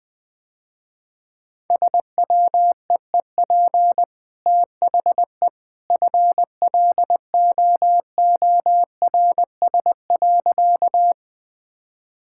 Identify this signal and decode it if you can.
Morse code, heard as a keyed tone, about 20 wpm, 700 Hz. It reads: SWEEP THE FLOORS.
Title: Easy Sentences with Hints - 20wpm